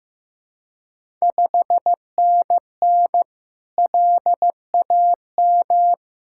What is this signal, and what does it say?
Morse code, heard as a keyed tone, about 15 wpm, 700 Hz. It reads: 5NN LAM
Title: Ontario QSO Party Exchange 15wpm